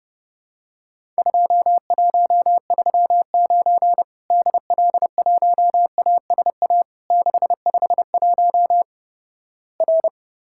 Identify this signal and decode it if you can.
Morse code, heard as a keyed tone, about 32 wpm, 700 Hz. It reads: 2139 DL1ASA 651 R
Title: Worked All Europe DX Contest - QTC Exchange - 30wpm